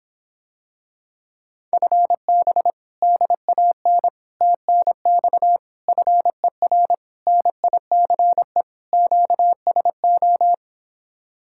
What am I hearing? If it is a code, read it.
Morse code, heard as a keyed tone, about 26 wpm, 700 Hz. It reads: FB DAN TNX FER NICE QSO